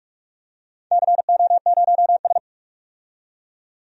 Morse code, heard as a keyed tone, about 44 wpm, 690 Hz. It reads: CO0S
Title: DXCC-100 Call Signs - 4 Characters - 45wpm